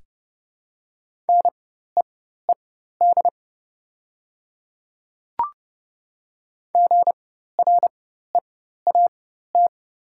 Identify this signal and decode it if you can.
Morse code, heard as a keyed tone, about 29 wpm, spaced as for 14 wpm, 700 Hz. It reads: NEED GREAT
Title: 4x Character Spacing - Top 300 Words - 30wpm